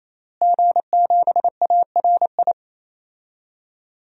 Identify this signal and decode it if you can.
Morse code, heard as a keyed tone, about 28 wpm, 700 Hz. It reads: G7ARI
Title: DXCC-100 Call Signs - 5 Characters - Repeated 3x - 28wpm